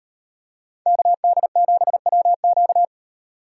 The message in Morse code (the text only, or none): KD7WQ